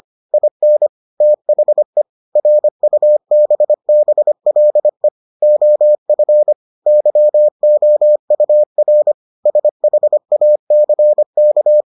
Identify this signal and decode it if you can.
Morse code, heard as a keyed tone, about 25 wpm, 600 Hz. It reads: IN THE RUBBLE OF YOUR SHACK